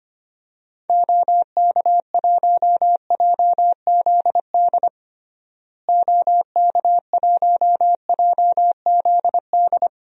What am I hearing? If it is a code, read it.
Morse code, heard as a keyed tone, about 25 wpm, 700 Hz. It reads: OK1JZD OK1JZD
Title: DXCC-100 Call Signs - 6 Characters - Repeated 2x - 25wpm